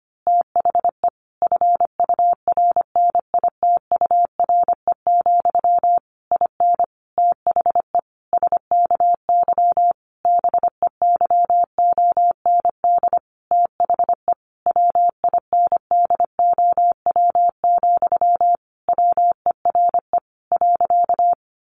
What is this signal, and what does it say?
Morse code, heard as a keyed tone, about 25 wpm, 700 Hz. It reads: THE FURNITURE, IN THE SKY BEYOND THE WINDOW, WERE .
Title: A Sound Of Thunder 25wpm